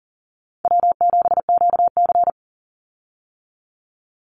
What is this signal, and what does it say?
Morse code, heard as a keyed tone, about 40 wpm, 700 Hz. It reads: W7QC
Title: Easy Call Signs Repeated 2x - 40wpm